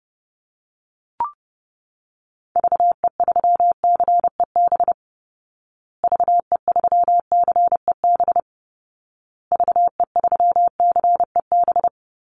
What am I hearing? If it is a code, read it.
Morse code, heard as a keyed tone, about 30 wpm, 700 Hz. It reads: VE3CEB VE3CEB VE3CEB
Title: Call Signs Repeated 2x - 30wpm